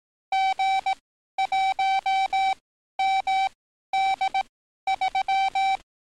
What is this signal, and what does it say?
Morse code, heard as a keyed tone, about 18 wpm, spaced as for 12 wpm, 770 Hz. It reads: G1MD3
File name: training / NAVPERS 11049to11060 / NAVPERS 11052 RA RB